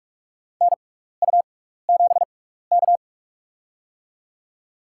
Morse code, heard as a keyed tone, about 46 wpm, spaced as for 16 wpm, 700 Hz. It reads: NU7K
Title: USA Call Signs - 4 Characters - 6x Character Spacing - 45wpm